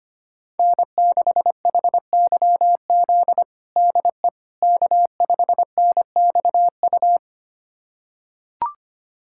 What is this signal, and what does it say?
Morse code, heard as a keyed tone, about 25 wpm, 700 Hz. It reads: N6HYZ DE K5NXU
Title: USA Callsign DE Callsign - 5 Characters - 25wpm